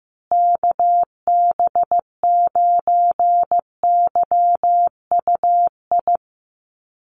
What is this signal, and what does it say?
Morse code, heard as a keyed tone, about 15 wpm, 700 Hz. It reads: KB9YUI
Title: USA Call Signs Repeated 3x 15wpm